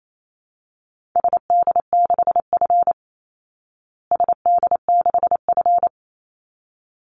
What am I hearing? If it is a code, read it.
Morse code, heard as a keyed tone, about 28 wpm, 700 Hz. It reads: SD6F SD6F
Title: Call Signs Repeated 3x - 28wpm